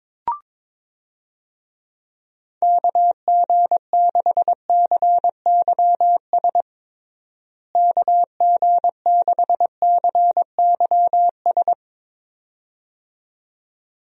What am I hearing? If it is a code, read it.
Morse code, heard as a keyed tone, about 22 wpm, 700 Hz. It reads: KG6CYS KG6CYS
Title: USA Call Signs - 6 Characters - Repeated 2x - 22wpm